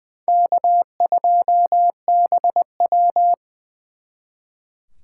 Morse code, heard as a keyed tone, about 24 wpm, 700 Hz. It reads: K2BW
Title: Speed Racing - USA Call Signs 20wpm